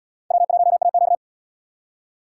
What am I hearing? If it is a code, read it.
Morse code, heard as a keyed tone, about 72 wpm, 700 Hz. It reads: W1AQ